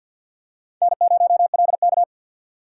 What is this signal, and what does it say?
Morse code, heard as a keyed tone, about 50 wpm, 700 Hz. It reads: N0LK
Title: USA Call Signs - 4 Characters - 3x Character Spacing - 50wpm